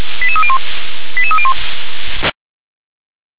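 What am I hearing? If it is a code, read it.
Five-tone selective call (ZVEI), heard as quick runs of five tones, tones of about 70 ms. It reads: 88301 88301
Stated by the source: ZVEI five-tone selective call received on SDR, code 88598.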